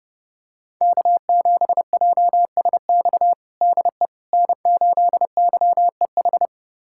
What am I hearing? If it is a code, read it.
Morse code, heard as a keyed tone, about 32 wpm, 700 Hz. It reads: K7JSX DE N8YEH
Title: USA Callsign DE Callsign - 5 Characters - 30wpm